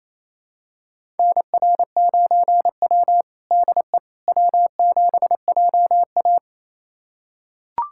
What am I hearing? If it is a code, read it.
Morse code, heard as a keyed tone, about 28 wpm, 700 Hz. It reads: NR9W DE W7JA
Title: USA Callsign DE Callsign - 4 Characters - 28wpm